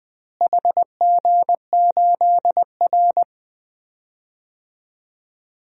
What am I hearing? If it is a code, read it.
Morse code, heard as a keyed tone, about 20 wpm, 700 Hz. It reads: HG8R
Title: DXCC-100 Call Signs - 4 Characters - Repeated 2x - 20wpm